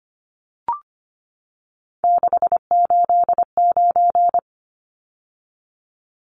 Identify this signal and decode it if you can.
Morse code, heard as a keyed tone, about 25 wpm, 700 Hz. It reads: 689
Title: Numbers - 3-Digit 25wpm